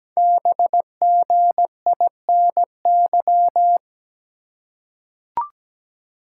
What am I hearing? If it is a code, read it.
Morse code, heard as a keyed tone, about 17 wpm, 700 Hz. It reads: BGINY
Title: Code Groups - 5 Letters - 17wpm